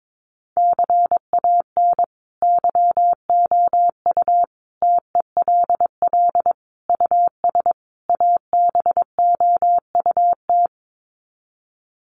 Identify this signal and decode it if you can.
Morse code, heard as a keyed tone, about 22 wpm, 700 Hz. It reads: CAN YOU TELL US ABOUT
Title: Sets of 5 Words Encore2 22wpm